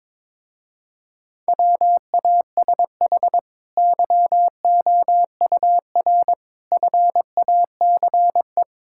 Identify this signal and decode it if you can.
Morse code, heard as a keyed tone, about 22 wpm, 700 Hz. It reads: WASH YOUR FACE
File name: practice / 22wpm / Easy Sentences with Hints - 22wpm